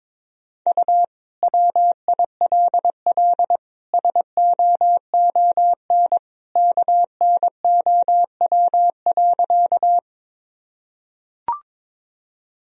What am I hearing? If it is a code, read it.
Morse code, heard as a keyed tone, about 23 wpm, 700 Hz. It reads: U WILL SOON KNOW.
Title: Sentences from Top 400 Words and Abbreviations 22wpm